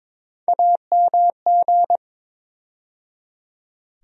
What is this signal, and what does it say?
Morse code, heard as a keyed tone, about 22 wpm, 700 Hz. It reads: AMG
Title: USA Call Sign Postfixes - 3 Letters 22wpm